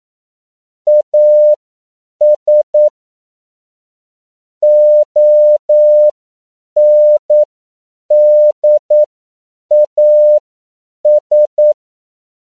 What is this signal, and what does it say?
Morse code, heard as a keyed tone, about 9 wpm, spaced as for 7 wpm, 600 Hz. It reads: AS ONDAS
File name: CW / 10PPM